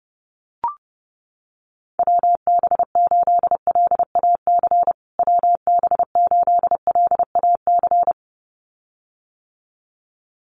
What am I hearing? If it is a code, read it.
Morse code, heard as a keyed tone, about 30 wpm, 700 Hz. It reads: WB8LAC WB8LAC
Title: USA Call Signs Repeated 2x 30wpm